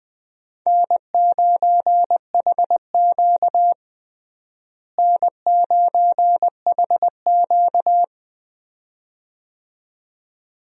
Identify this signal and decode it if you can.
Morse code, heard as a keyed tone, about 20 wpm, 700 Hz. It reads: N9HQ N9HQ
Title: USA Call Signs - 4 Characters - Repeated 2x - 20wpm